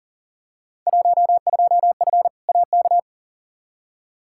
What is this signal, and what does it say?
Morse code, heard as a keyed tone, about 40 wpm, 700 Hz. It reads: 12F AK
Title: Field Day Exchanges Encore - 40wpm